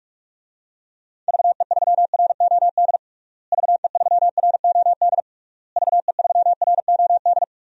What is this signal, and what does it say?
Morse code, heard as a keyed tone, about 47 wpm, 700 Hz. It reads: VE3ROD VE3ROD VE3ROD
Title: DXCC-100 Call Signs - 6 Characters - Repeated 3x - 45wpm